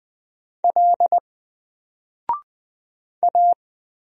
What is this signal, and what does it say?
Morse code, heard as a keyed tone, about 20 wpm, 700 Hz. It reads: L A
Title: Single Letter-Number 20wpm